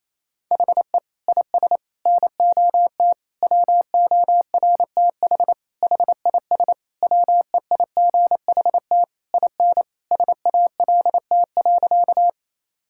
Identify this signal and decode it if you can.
Morse code, heard as a keyed tone, about 28 wpm, 700 Hz. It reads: HE IS NOT WORTH HIS WEIGHT IN SALT.
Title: Sentences from Top 1000 Words - Review Part 5 - 28wpm